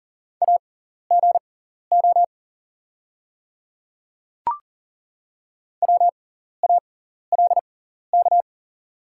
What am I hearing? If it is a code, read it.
Morse code, heard as a keyed tone, about 40 wpm, spaced as for 14 wpm, 690 Hz. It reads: AGO WALK
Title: 6x Character Spacing - Top 500 Words - 40wpm